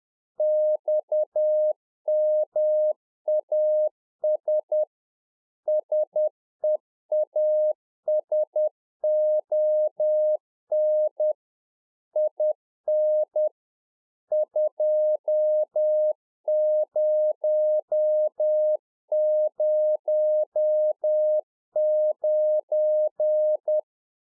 Morse code, heard as a keyed tone, about 10 wpm, 610 Hz. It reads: XMAS SEASON IN 2009